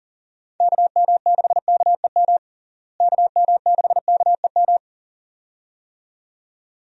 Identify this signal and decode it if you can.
Morse code, heard as a keyed tone, about 37 wpm, 700 Hz. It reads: KM6KEM KM6KEM
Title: USA Call Signs - 6 Characters - Repeated 2x - 40wpm